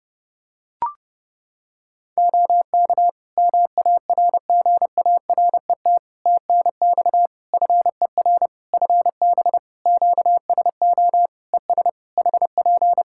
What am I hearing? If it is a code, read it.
Morse code, heard as a keyed tone, about 30 wpm, 700 Hz. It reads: OK MARGARET TNX FER FB QSO ES HP
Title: Practice QSO - Exchange 4 - Ending - 30wpm